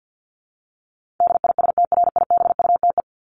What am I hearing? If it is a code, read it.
Morse code, heard as a keyed tone, about 50 wpm, 700 Hz. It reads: DISTRIBUTE
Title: Top 1501-1750 Words 50wpm